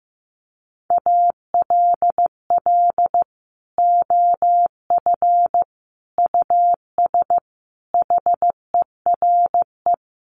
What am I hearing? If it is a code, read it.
Morse code, heard as a keyed tone, about 15 wpm, 700 Hz. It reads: ALL OF US HERE
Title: Sets of 5 Words 15wpm